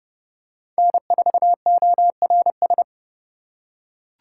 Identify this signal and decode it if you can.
Morse code, heard as a keyed tone, about 31 wpm, 700 Hz. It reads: N4ORS